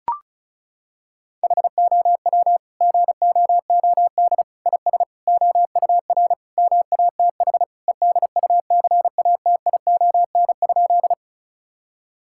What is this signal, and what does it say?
Morse code, heard as a keyed tone, about 36 wpm, 700 Hz. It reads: HOW GOOD IS OUR MATH EDUCATION?
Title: Sentences from Top 1000 Words - Review Part 6 - 35wpm